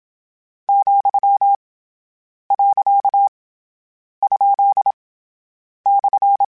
Morse code, heard as a keyed tone, about 27 wpm, 800 Hz. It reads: , . ? /
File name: Misc / Misc-mp3s / Words @25